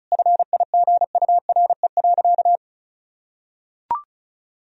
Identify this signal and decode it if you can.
Morse code, heard as a keyed tone, about 34 wpm, 700 Hz. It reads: FIGURE.
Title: Sentences from Top 1000 Words - Review Part 2 - 35wpm